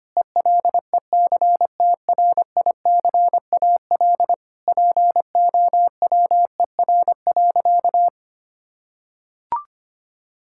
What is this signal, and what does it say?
Morse code, heard as a keyed tone, about 25 wpm, 700 Hz. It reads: ELECTRICAL POWER.
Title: Sentences from Top 1000 Words - Review Part 6 - 25wpm